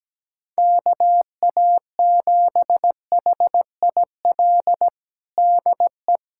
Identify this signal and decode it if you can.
Morse code, heard as a keyed tone, about 17 wpm, 700 Hz. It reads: KA7HIL DE
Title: USA Callsign DE Callsign - 6 Characters - 17wpm